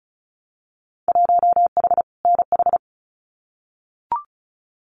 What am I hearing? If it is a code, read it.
Morse code, heard as a keyed tone, about 36 wpm, 700 Hz. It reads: 1H NH
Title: Winter Field Day Exchanges 35wpm